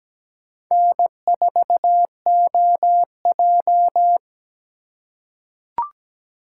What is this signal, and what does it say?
Morse code, heard as a keyed tone, about 17 wpm, 700 Hz. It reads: N4OJ
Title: USA Call Signs - 3x Character Spacing - 17wpm